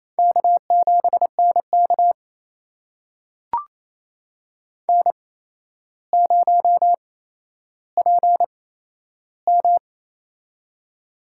Morse code, heard as a keyed tone, about 28 wpm, spaced as for 12 wpm, 700 Hz. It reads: K7NK N0PM
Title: USA Call Signs - 4 Characters - 8x Character Spacing - 28wpm